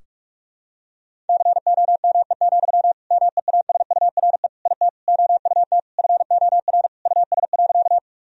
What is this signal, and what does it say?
Morse code, heard as a keyed tone, about 44 wpm, 700 Hz. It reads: COME, MEASURE IT OUT FOR US.
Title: Sentences from Top 1000 Words - Review Part 2 - 45wpm